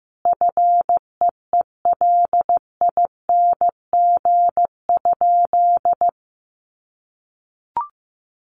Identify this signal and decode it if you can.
Morse code, heard as a keyed tone, about 15 wpm, 700 Hz. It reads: FEELING?
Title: Sentences from Top 1200 Words 15wpm